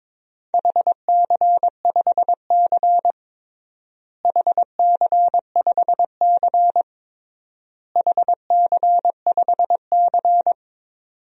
Morse code, heard as a keyed tone, about 22 wpm, 700 Hz. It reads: HC5C HC5C HC5C
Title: DXCC-100 Call Signs - 4 Characters - Repeated 3x - 22wpm